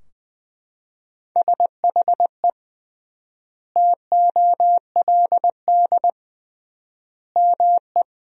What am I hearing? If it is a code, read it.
Morse code, heard as a keyed tone, about 20 wpm, 700 Hz. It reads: SHE TOLD ME
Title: 3x Word-Spacing - Sentences from Top 400 Words - 20wpm